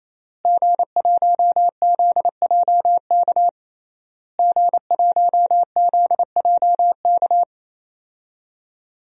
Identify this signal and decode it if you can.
Morse code, heard as a keyed tone, about 28 wpm, 700 Hz. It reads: G1ZJK G1ZJK
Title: DXCC-100 Call Signs - 5 Characters - Repeated 3x - 28wpm